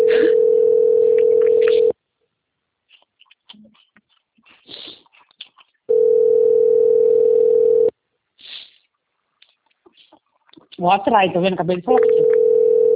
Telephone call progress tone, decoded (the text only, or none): ringback tone